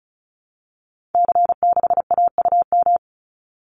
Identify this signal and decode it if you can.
Morse code, heard as a keyed tone, about 34 wpm, 700 Hz. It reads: C6AUM
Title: Call Signs Repeated 3x - 35wpm